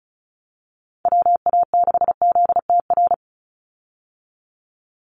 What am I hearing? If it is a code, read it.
Morse code, heard as a keyed tone, about 34 wpm, 700 Hz. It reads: WA6ZTR
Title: USA Call Signs 35wpm